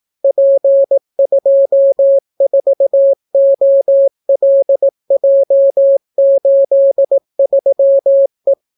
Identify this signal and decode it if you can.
Morse code, heard as a keyed tone, about 18 wpm, 550 Hz. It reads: P24OLJ83E